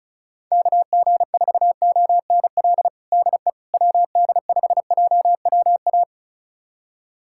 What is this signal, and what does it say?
Morse code, heard as a keyed tone, about 34 wpm, 700 Hz. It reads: KG4ONL DE WD5JWA
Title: USA Callsign DE Callsign - 6 Characters - 35wpm